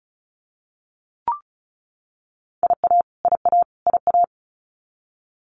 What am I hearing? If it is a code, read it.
Morse code, heard as a keyed tone, about 34 wpm, 700 Hz. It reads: IA IA IA
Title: US States Repeated 3x 35wpm